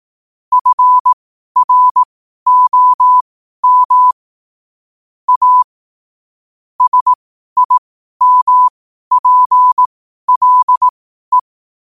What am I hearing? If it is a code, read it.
Morse code, heard as a keyed tone, about 18 wpm, spaced as for 12 wpm, 1000 Hz. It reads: FROM A SIMPLE
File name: qotdfiles / 12wpm